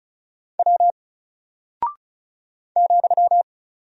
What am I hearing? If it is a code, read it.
Morse code, heard as a keyed tone, about 36 wpm, 700 Hz. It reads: W ,
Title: Single Character (Letter:Number:Punctuation:Prosign) Alt 35wpm